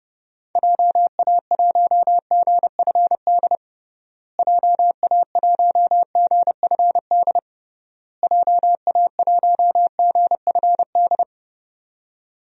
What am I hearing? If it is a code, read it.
Morse code, heard as a keyed tone, about 30 wpm, 700 Hz. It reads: JA1GFD JA1GFD JA1GFD
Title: DXCC-100 Call Signs - 6 Characters - Repeated 3x - 30wpm